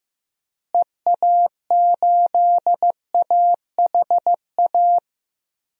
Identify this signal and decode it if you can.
Morse code, heard as a keyed tone, about 16 wpm, 700 Hz. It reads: EA8AHA
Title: DXCC-100 Call Signs - 6 Characters - Repeated 2x - 15wpm